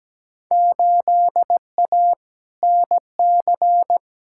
Morse code, heard as a keyed tone, about 17 wpm, 700 Hz. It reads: 8A NC
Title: Field Day Exchanges - 17wpm